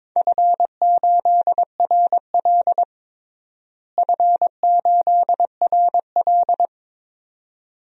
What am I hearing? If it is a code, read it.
Morse code, heard as a keyed tone, about 21 wpm, 700 Hz. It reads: F8RL F8RL
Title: DXCC-100 Call Signs - 4 Characters - Repeated 2x - 22wpm